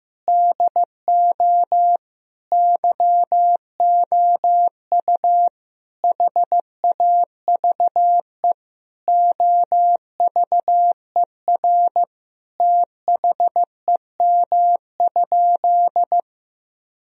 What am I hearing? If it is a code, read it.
Morse code, heard as a keyed tone, about 14 wpm, 700 Hz. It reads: DO YOU HAVE OVER THEM?
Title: Sentences from Top 1000 Words - Review Part 3 - 15wpm